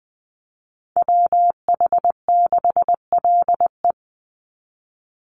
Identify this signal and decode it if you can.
Morse code, heard as a keyed tone, about 20 wpm, 700 Hz. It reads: WH6LE
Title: NA SOTA Chaser Callsigns 20wpm